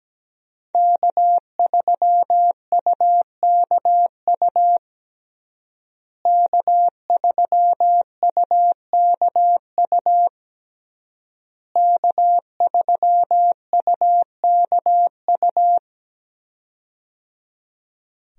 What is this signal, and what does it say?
Morse code, heard as a keyed tone, about 17 wpm, 700 Hz. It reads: K3UKU K3UKU K3UKU
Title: USA Call Signs - 5 Characters - Repeated 3x - 17wpm